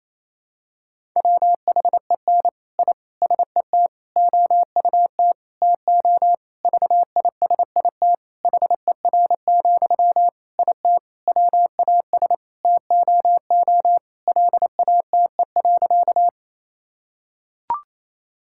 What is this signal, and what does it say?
Morse code, heard as a keyed tone, about 28 wpm, 700 Hz. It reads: WHEN I SET OUT TO VISIT HER, IT WAS TOO LATE.
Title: Sentences from Top 1000 Words - Review Part 6 - 28wpm